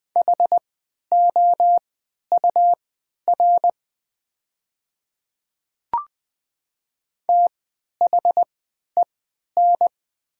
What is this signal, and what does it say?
Morse code, heard as a keyed tone, about 20 wpm, spaced as for 11 wpm, 700 Hz. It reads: HOUR THEN
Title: 3x Character Spacing - Top 500 Words - 20wpm